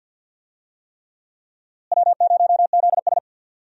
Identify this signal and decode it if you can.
Morse code, heard as a keyed tone, about 47 wpm, 700 Hz. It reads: W0ZS
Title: USA Call Signs - 4 Characters - Repeated 3x - 50wpm